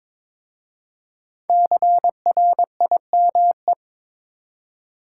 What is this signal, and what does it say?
Morse code, heard as a keyed tone, about 22 wpm, 690 Hz. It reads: CRIME
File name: 5 Letter Words - First 3 Letters Given - 22wpm